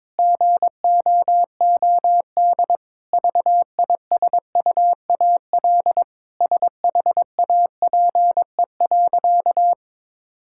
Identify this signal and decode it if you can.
Morse code, heard as a keyed tone, about 22 wpm, 700 Hz. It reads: GOOD VISUAL SHAPE.
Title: Sentences from Top 1000 Words - Review Part 6 - 22wpm